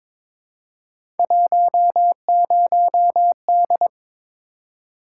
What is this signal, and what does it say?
Morse code, heard as a keyed tone, about 21 wpm, 700 Hz. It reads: 10D
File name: practice / 22wpm / Three Letter-Number Combinations - 22wpm